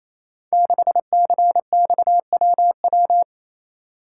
Morse code, heard as a keyed tone, about 27 wpm, 690 Hz. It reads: 6CXWW
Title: Code Groups - 5 Letter:Numbers - 28wpm